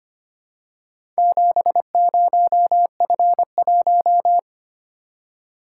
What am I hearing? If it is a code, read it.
Morse code, heard as a keyed tone, about 26 wpm, 690 Hz. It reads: 70F1
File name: practice / 25wpm / Four Letter-Number Combinations - 25wpm